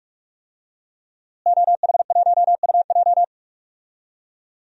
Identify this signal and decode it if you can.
Morse code, heard as a keyed tone, about 43 wpm, 700 Hz. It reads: OH1UJ
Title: DXCC-100 Call Signs - 5 Characters - Repeated 3x - 45wpm